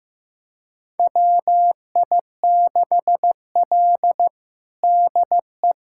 Morse code, heard as a keyed tone, about 16 wpm, 700 Hz. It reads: WI6L DE
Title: USA Callsign DE Callsign - 4 Characters - 15wpm